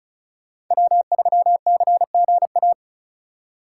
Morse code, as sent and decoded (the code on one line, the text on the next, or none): .-- ...-- -.-. --. .-
W3CGA